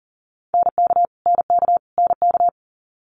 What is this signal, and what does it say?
Morse code, heard as a keyed tone, about 40 wpm, 700 Hz. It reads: NK NK NK
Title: Multi-letter Phonemes Repeated 3x 40wpm